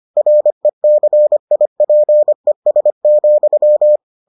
Morse code, heard as a keyed tone, about 25 wpm, 600 Hz. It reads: RECIPES,